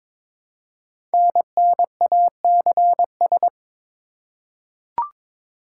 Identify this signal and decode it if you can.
Morse code, heard as a keyed tone, about 23 wpm, 700 Hz. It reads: NNACS
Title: Code Groups - 5 Letters - 22wpm